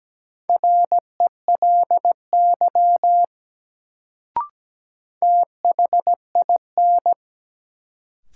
Morse code, heard as a keyed tone, about 17 wpm, 700 Hz. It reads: RELY THIN
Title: Sentences from Top 1200 Words - New Word Drill - 17wpm